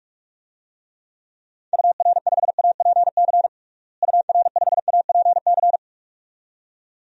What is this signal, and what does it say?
Morse code, heard as a keyed tone, about 46 wpm, 700 Hz. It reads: UR5APC UR5APC
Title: DXCC-100 Call Signs - 6 Characters - Repeated 2x - 45wpm